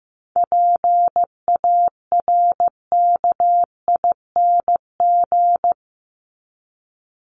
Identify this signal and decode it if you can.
Morse code, heard as a keyed tone, about 15 wpm, 700 Hz. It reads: PARKING
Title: Top 1501-1750 Words 15wpm